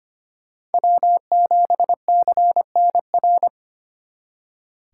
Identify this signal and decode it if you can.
Morse code, heard as a keyed tone, about 32 wpm, 700 Hz. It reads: W7CNR